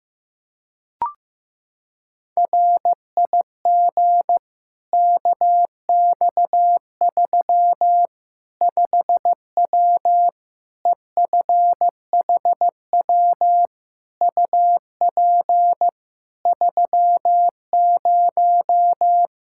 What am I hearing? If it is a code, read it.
Morse code, heard as a keyed tone, about 15 wpm, 700 Hz. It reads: RIG KX3 5W EFHW UP 30
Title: Practice QSO - Exchange 2 - Info - 15wpm